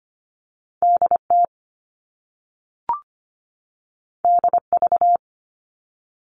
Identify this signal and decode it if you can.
Morse code, heard as a keyed tone, about 25 wpm, 700 Hz. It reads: DT DV
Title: All Cut Numbers - 2-Digit 25wpm